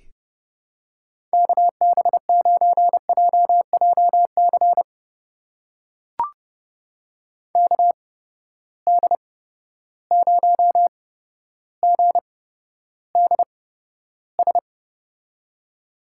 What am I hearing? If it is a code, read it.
Morse code, heard as a keyed tone, about 32 wpm, spaced as for 13 wpm, 700 Hz. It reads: KB9JJC KD0GDS
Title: USA Call Signs - 8x Character Spacing - 30wpm